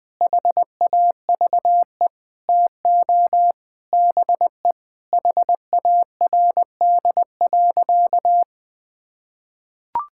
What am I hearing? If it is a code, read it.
Morse code, heard as a keyed tone, about 20 wpm, 700 Hz. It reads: HAVE TO BE HARD.
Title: Sentences with 4-Word Phrases - 20wpm